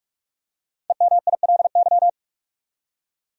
Morse code, heard as a keyed tone, about 41 wpm, 700 Hz. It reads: EMILY
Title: USA Common Names - 45wpm